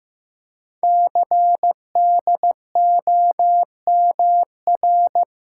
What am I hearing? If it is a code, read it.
Morse code, heard as a keyed tone, about 15 wpm, 700 Hz. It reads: CDOMR
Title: Code Groups - 5 Letters - 15wpm